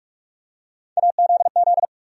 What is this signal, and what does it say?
Morse code, heard as a keyed tone, about 46 wpm, 700 Hz. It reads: AZZ
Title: USA Call Sign Postfixes - 3 Letters 45wpm